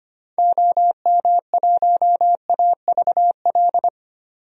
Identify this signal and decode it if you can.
Morse code, heard as a keyed tone, about 25 wpm, 700 Hz. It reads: OM1AVL
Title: DXCC-100 Call Signs - 6 Characters - 25wpm